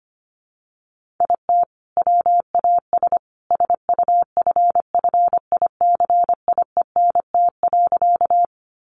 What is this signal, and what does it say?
Morse code, heard as a keyed tone, about 25 wpm, 700 Hz. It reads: IT WAS SUFFICIENT.
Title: Sentences from Top 1400 Words 25wpm